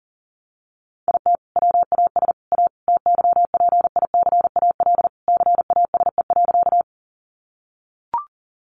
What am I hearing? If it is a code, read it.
Morse code, heard as a keyed tone, about 40 wpm, 700 Hz. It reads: IT WAS A TYPICAL CASE.
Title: Sentences from Top 1100 Words 40wpm